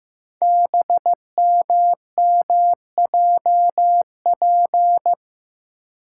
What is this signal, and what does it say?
Morse code, heard as a keyed tone, about 15 wpm, 700 Hz. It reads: BMMJP
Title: Code Groups - 5 Letter:Numbers - 15wpm